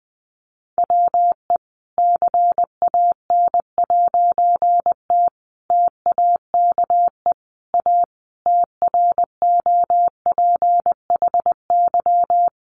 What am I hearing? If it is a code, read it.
Morse code, heard as a keyed tone, about 20 wpm, 700 Hz. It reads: WE CAN'T TAKE A TROPHY
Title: A Sound Of Thunder 20wpm